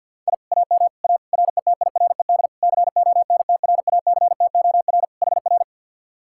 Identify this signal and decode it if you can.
Morse code, heard as a keyed tone, about 65 wpm, 700 Hz. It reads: I AM A RETIRED CONTRACTOR HR